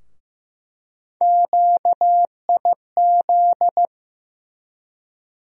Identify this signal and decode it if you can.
Morse code, heard as a keyed tone, about 15 wpm, 700 Hz. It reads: QIZ
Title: USA Call Sign Postfixes - 3 Letters 15wpm